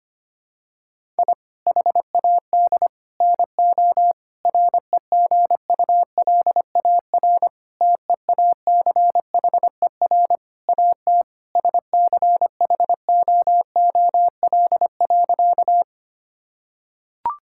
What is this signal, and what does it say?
Morse code, heard as a keyed tone, about 25 wpm, 700 Hz. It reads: I HAD NO REGULAR TEACHER AT SCHOOL.
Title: Sentences from Top 1000 Words - Review Part 6 - 25wpm